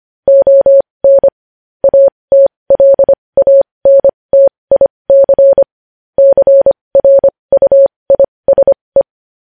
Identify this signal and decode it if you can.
Morse code, heard as a keyed tone, about 25 wpm, spaced as for 20 wpm, 550 Hz. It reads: ON ATLANTIC CRUISE
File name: podcast / News headlines-1x-25x20